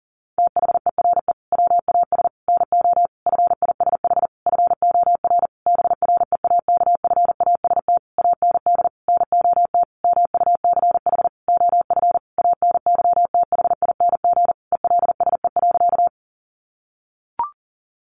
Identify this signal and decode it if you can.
Morse code, heard as a keyed tone, about 37 wpm, 700 Hz. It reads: THERE WAS NO FISH FOR BREAKFAST AND NOT MUCH OF ANYTHING ELSE.
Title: Sentences from Top 1500 Words 40wpm